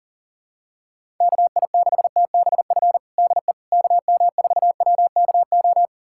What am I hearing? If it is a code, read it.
Morse code, heard as a keyed tone, about 40 wpm, 700 Hz. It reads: KI6TBF DE KM4WKO
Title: USA Callsign DE Callsign 40wpm